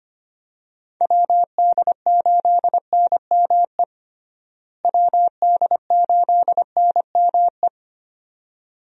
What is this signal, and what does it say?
Morse code, heard as a keyed tone, about 25 wpm, 700 Hz. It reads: WD8NME WD8NME
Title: USA Call Signs - 6 Characters - Repeated 2x - 25wpm